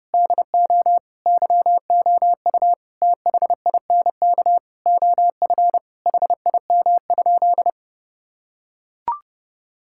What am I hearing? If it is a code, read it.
Morse code, heard as a keyed tone, about 30 wpm, 700 Hz. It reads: DO YOU THINK OF HIM?
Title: Sentences from Top 1000 Words - Review Part 5 - 30wpm